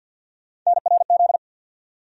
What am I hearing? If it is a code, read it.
Morse code, heard as a keyed tone, about 53 wpm, 700 Hz. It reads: NRZ
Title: USA Call Sign Postfixes - 3 Letters 50wpm